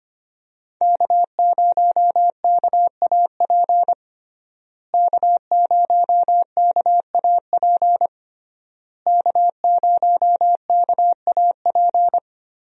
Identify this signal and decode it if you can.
Morse code, heard as a keyed tone, about 25 wpm, 700 Hz. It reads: K0KAP K0KAP K0KAP
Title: USA Call Signs - 5 Characters - Repeated 3x - 25wpm